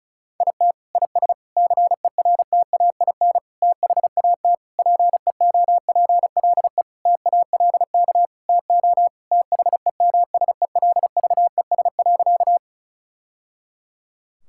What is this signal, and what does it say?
Morse code, heard as a keyed tone, about 34 wpm, 700 Hz. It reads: IT IS CERTAIN THAT PEOPLE TALK TO THEMSELVES.
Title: Sentences from Top 1000 Words - Review Part 5 - 35wpm